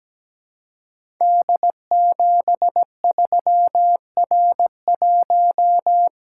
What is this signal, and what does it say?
Morse code, heard as a keyed tone, about 17 wpm, 700 Hz. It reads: D73R1
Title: Code Groups - 5 Letter:Numbers - 17wpm